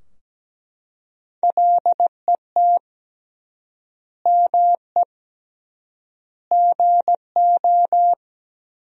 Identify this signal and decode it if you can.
Morse code, heard as a keyed tone, about 17 wpm, 700 Hz. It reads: LET ME GO
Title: 3x Word-Spacing - Sentences from Top 300 Words - 17wpm